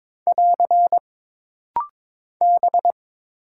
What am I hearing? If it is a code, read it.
Morse code, heard as a keyed tone, about 22 wpm, 700 Hz. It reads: <AR> B
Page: This rapid-fire timing is geared to push your ICR (Instant Character Recognition) skills.